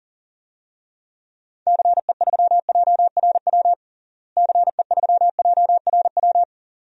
Morse code, heard as a keyed tone, about 40 wpm, 700 Hz. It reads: CE3JRW CE3JRW
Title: DXCC-100 Call Signs - 6 Characters - Repeated 2x - 40wpm